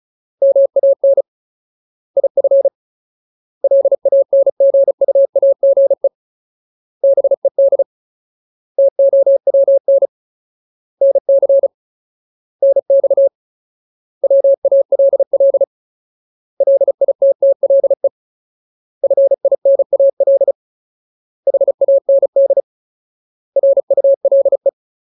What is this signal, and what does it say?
Morse code, heard as a keyed tone, about 34 wpm, 550 Hz. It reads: MAN IF LANGUAGE BED TOWN NC NX WALL LITTLE FINAL HAND RULE
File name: Ultimate Word List / 35WPM Ultimate